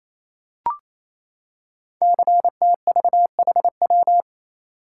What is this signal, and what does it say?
Morse code, heard as a keyed tone, about 27 wpm, 700 Hz. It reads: CTVHW
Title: Code Groups - 5 Letters - 28wpm